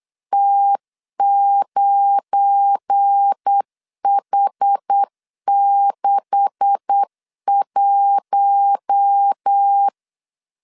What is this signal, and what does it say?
Morse code, heard as a keyed tone, about 8 wpm, 800 Hz. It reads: T9H61
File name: 8wpm